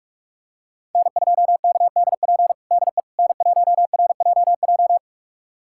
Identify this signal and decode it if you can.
Morse code, heard as a keyed tone, about 46 wpm, 700 Hz. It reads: N2KDP DE N1RJJ
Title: USA Callsign DE Callsign - 5 Characters - 45wpm